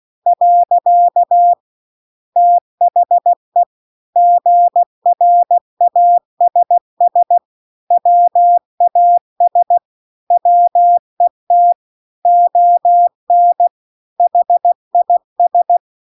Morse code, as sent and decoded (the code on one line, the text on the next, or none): .-.-.- / - .... / --. .-. .- ... ... / .-- .- ... / .-- . - / --- -. / .... .. ...
. THE GRASS WAS WET ON HIS